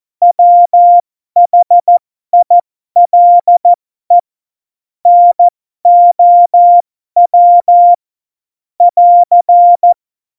Morse code, heard as a keyed tone, about 14 wpm, spaced as for 12 wpm, 700 Hz. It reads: WHILE NOW <AR>
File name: Morsekurs / Witze